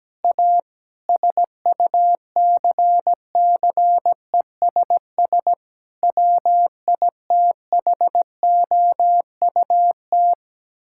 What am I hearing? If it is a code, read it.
Morse code, heard as a keyed tone, about 17 wpm, 700 Hz. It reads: A SUCCESS WITHOUT
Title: Sentences from Top 1000 Words - Review Part 6 - 17wpm